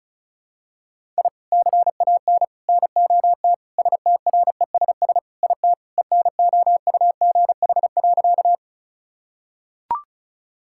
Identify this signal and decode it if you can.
Morse code, heard as a keyed tone, about 36 wpm, 700 Hz. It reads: I CAN NOT STRESS IT ENOUGH.
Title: Sentences from Top 1000 Words - Review Part 6 - 35wpm